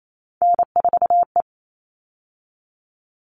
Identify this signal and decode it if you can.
Morse code, heard as a keyed tone, about 28 wpm, 700 Hz. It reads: N4E
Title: All Cut Numbers - 3-Digit 28wpm